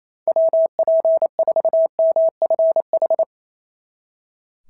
Morse code, heard as a keyed tone, about 27 wpm, 660 Hz. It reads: WP4MFH
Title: SOTA QSO Practice 28wpm